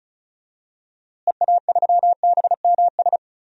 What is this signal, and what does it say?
Morse code, heard as a keyed tone, about 34 wpm, 700 Hz. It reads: EA3BMS
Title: DXCC-100 Call Signs - 6 Characters - Repeated 2x - 35wpm